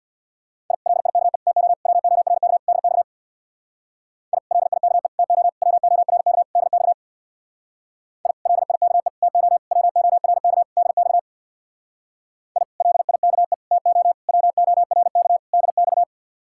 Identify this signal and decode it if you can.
Morse code, heard as a keyed tone, about 64 wpm, 700 Hz. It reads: I LIKE TO WORK DX I LIKE TO WORK DX I LIKE TO WORK DX I LIKE TO WORK DX